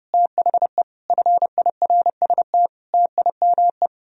Morse code, heard as a keyed tone, about 40 wpm, spaced as for 35 wpm, 700 Hz. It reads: THE FIRST TIME
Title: Speed-Racing - Sets of 5 Words 30wpm